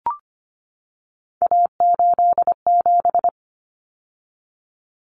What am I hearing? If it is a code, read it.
Morse code, heard as a keyed tone, about 25 wpm, 700 Hz. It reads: A87
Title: Common Cut Numbers - 3-Digit 25wpm